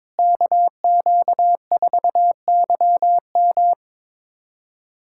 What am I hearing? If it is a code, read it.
Morse code, heard as a keyed tone, about 22 wpm, 700 Hz. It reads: KQ4YM
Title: USA Callsign DE Callsign 22wpm